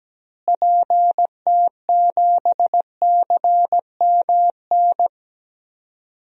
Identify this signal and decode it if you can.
Morse code, heard as a keyed tone, about 17 wpm, 700 Hz. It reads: PT7CMN
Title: DXCC-100 Call Signs - 6 Characters - Repeated 2x - 17wpm